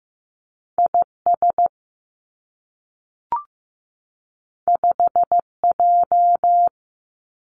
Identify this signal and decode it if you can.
Morse code, heard as a keyed tone, about 15 wpm, 700 Hz. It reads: I S 5 J
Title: Two Challenging Letter-Number Combinations 15wpm